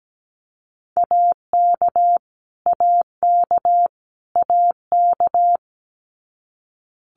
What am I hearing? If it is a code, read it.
Morse code, heard as a keyed tone, about 17 wpm, 700 Hz. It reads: AK AK AK
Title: US States Repeated 3x 17wpm